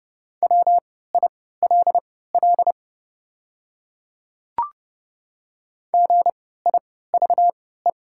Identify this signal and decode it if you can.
Morse code, heard as a keyed tone, about 32 wpm, spaced as for 17 wpm, 700 Hz. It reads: WILL GIVE
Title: 3x Character Spacing - Top 200 Words - 30wpm